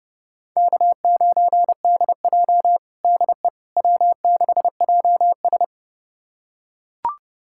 Morse code, heard as a keyed tone, about 30 wpm, 700 Hz. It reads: K9DJ DE W6JS
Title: USA Callsign DE Callsign - 4 Characters - 30wpm